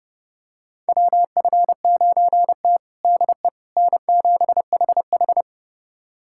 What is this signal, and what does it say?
Morse code, heard as a keyed tone, about 30 wpm, 700 Hz. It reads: WF9T DE N7HH
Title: USA Callsign DE Callsign - 4 Characters - 30wpm